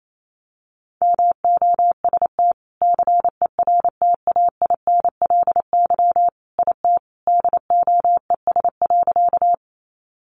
Morse code, heard as a keyed tone, about 28 wpm, 700 Hz. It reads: MOST CERTAINLY IT DOES.
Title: Sentences from Top 900 Words 28wpm